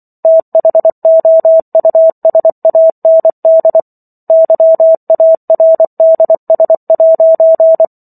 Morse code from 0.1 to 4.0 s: THOUSAND

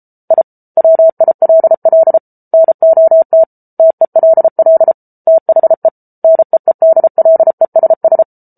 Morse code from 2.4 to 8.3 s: NOT TELL THE NEEDLESS

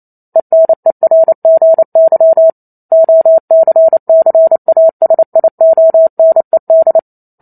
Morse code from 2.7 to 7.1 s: OCCASIONED

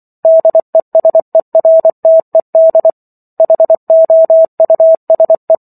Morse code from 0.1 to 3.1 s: DESERTED

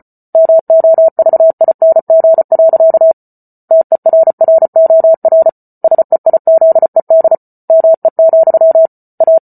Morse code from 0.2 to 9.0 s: MOVING. TERROR SEIZED ME,

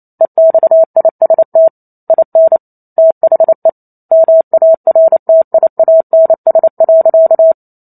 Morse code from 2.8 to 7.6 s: THE MARTIANS.

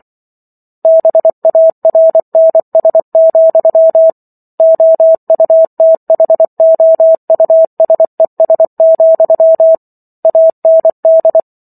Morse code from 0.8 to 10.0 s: BARNS, OUTHOUSES,

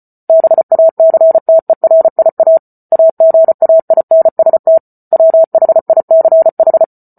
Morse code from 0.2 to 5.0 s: BACTERIA AGAINST

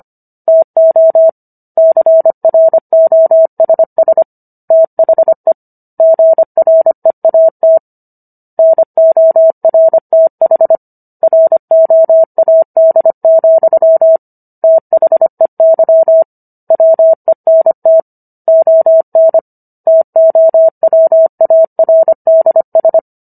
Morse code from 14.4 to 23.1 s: THEY WENT ON TOWARDS

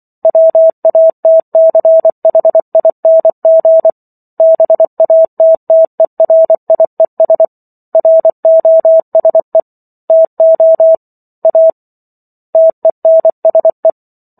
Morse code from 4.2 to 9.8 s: BATTERIES ROSE

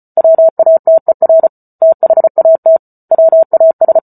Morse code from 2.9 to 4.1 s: WAS

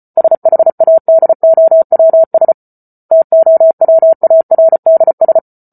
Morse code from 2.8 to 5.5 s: TOWARDS

